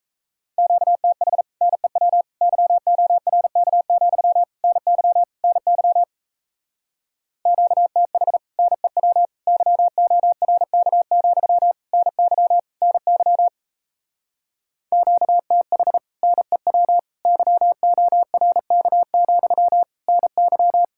Morse code, heard as a keyed tone, about 37 wpm, 700 Hz. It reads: QTH NEW YORK, NY NY QTH NEW YORK, NY NY QTH NEW YORK, NY